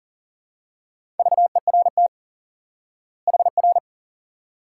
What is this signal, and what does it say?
Morse code, heard as a keyed tone, about 40 wpm, 700 Hz. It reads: VERT HR
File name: IQE Course 3 - Lesson 89 - YAGI - Two Elements - 40wpm